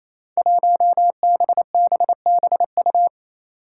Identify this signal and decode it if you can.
Morse code, heard as a keyed tone, about 28 wpm, 700 Hz. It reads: 1BBBU
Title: Five Challenging Letter-Number Combinations 28wpm